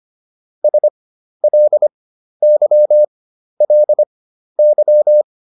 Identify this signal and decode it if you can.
Morse code, heard as a keyed tone, about 25 wpm, spaced as for 12 wpm, 600 Hz. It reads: SLYLY